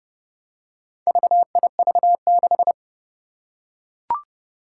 Morse code, heard as a keyed tone, about 30 wpm, 700 Hz. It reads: VIV6
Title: Four Challenging Letter-Number Combinations 30wpm